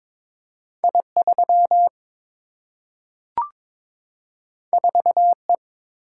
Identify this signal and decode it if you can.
Morse code, heard as a keyed tone, about 22 wpm, 700 Hz. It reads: I3 4E
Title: Single-Letter Single-Number 22wpm